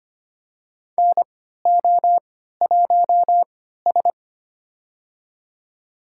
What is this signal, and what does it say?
Morse code, heard as a keyed tone, about 25 wpm, spaced as for 14 wpm, 700 Hz. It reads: NO1S
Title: USA Call Signs - 4 Characters - 3x Character Spacing - 25wpm